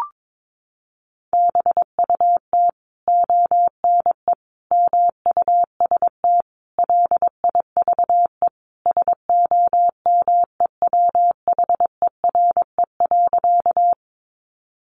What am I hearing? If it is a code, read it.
Morse code, heard as a keyed tone, about 22 wpm, 700 Hz. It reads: BUT ONE MUST LIVE SOMEWHERE.
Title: Sentences from Top 1400 Words 22wpm